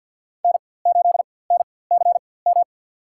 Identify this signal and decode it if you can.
Morse code, heard as a keyed tone, about 50 wpm, spaced as for 23 wpm, 700 Hz. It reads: N8NCM
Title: USA Call Signs - 5 Characters - 4x Character Spacing - 50wpm